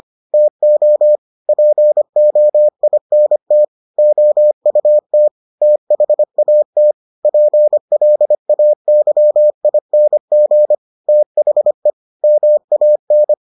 Morse code, heard as a keyed tone, about 25 wpm, 600 Hz. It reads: TO POINT OUT THAT PLAYING THE MAN